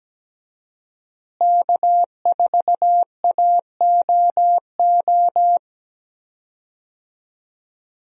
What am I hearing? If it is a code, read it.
Morse code, heard as a keyed tone, about 17 wpm, 700 Hz. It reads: K4AOO
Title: USA Call Signs - 5 Characters - Repeated 3x - 17wpm